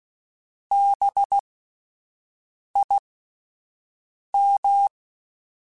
Morse code, heard as a keyed tone, about 16 wpm, spaced as for 5 wpm, 800 Hz. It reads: BIM
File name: Alphabetical / Alphabetical mp3s / Bs @5